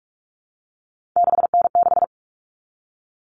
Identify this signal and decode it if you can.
Morse code, heard as a keyed tone, about 44 wpm, 700 Hz. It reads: 6N6
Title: All Cut Numbers - 3-Digit 45wpm